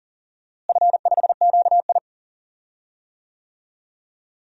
Morse code, heard as a keyed tone, about 40 wpm, 700 Hz. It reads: F5QI
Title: DXCC-100 Call Signs - 4 Characters - Repeated 3x - 40wpm